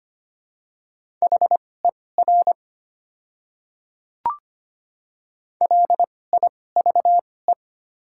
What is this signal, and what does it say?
Morse code, heard as a keyed tone, about 25 wpm, spaced as for 18 wpm, 700 Hz. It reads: HER LIVE
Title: 2x Character Spacing - Top 200 Words - 25wpm